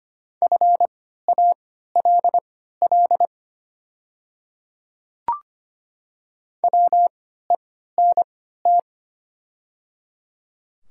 Morse code, heard as a keyed tone, about 25 wpm, spaced as for 14 wpm, 700 Hz. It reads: FALL WENT
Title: 3x Character Spacing - Top 500 Words - 25wpm